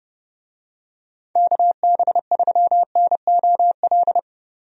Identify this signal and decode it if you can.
Morse code, heard as a keyed tone, about 28 wpm, 700 Hz. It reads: KB3NOL
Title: USA Call Signs - 6 Characters - Repeated 2x - 30wpm